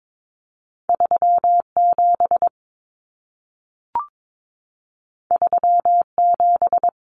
Morse code, heard as a keyed tone, about 22 wpm, 700 Hz. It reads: 37 37
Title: Numbers - 2-Digit 22wpm